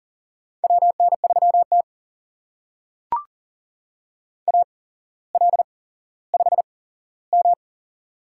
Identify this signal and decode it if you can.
Morse code, heard as a keyed tone, about 39 wpm, spaced as for 17 wpm, 700 Hz. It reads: WN3T AL5M
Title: USA Call Signs - 4 Characters - 8x Character Spacing - 40wpm